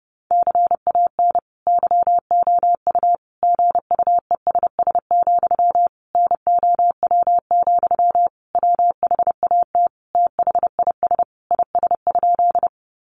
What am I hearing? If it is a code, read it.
Morse code, heard as a keyed tone, about 30 wpm, 700 Hz. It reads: CAN YOU GUESS, NOW, WHAT THIS IS?
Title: Sentences from Top 1500 Words 30wpm